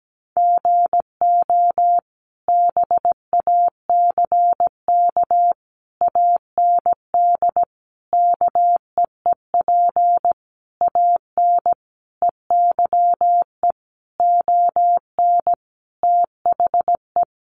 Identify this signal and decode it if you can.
Morse code, heard as a keyed tone, about 17 wpm, 700 Hz. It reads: GO BACK AND KEEP AN EYE ON THE
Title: Sentences from Top 300 Words 17wpm